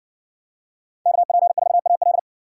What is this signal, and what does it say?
Morse code, heard as a keyed tone, about 60 wpm, 700 Hz. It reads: KP4AL